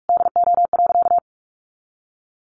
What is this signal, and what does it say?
Morse code, heard as a keyed tone, about 44 wpm, 700 Hz. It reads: DO.
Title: Sentences from Top 100 Words 45wpm